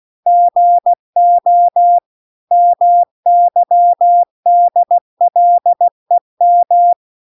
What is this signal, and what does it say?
Morse code, heard as a keyed tone, about 16 wpm, 700 Hz. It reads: GO MYDLEM
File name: recordings / 16wpm0000